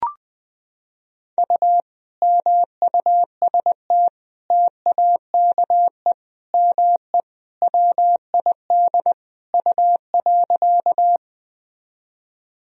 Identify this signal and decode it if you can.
Morse code, heard as a keyed tone, about 20 wpm, 700 Hz. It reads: U MUST TAKE ME WID U.
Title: Sentences from Top 200 Words and Abbreviations 20wpm